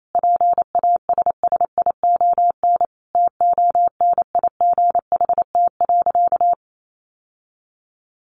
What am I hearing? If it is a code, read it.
Morse code, heard as a keyed tone, about 28 wpm, 700 Hz. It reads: PASSION TONIGHT.
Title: Sentences from Top 1300 Words 28wpm